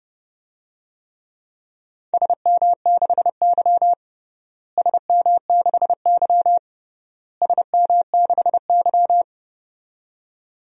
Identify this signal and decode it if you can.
Morse code, heard as a keyed tone, about 30 wpm, 700 Hz. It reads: SM6Y SM6Y SM6Y
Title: DXCC-100 Call Signs - 4 Characters - Repeated 3x - 30wpm